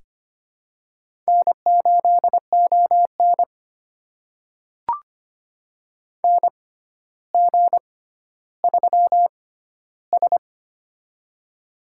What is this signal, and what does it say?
Morse code, heard as a keyed tone, about 25 wpm, spaced as for 13 wpm, 700 Hz. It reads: N8ON NG3S
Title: USA Call Signs - 6x Character Spacing - 25wpm